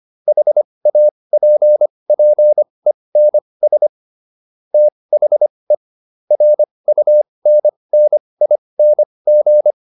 Morse code, heard as a keyed tone, about 25 wpm, spaced as for 20 wpm, 600 Hz. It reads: HAPPENS THE RUNNING